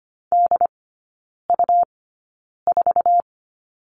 Morse code, heard as a keyed tone, about 25 wpm, spaced as for 9 wpm, 700 Hz. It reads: DU4
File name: Lesson 17e - Letter u - Three Characters 25wpm